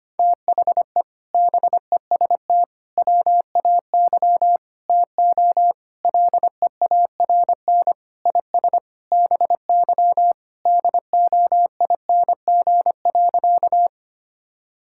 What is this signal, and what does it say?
Morse code, heard as a keyed tone, about 25 wpm, 700 Hz. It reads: THE BEST WAY TO LEARN IS BY DOING.
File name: practice / 25wpm / Sentences with 4-Word Phrases - 25wpm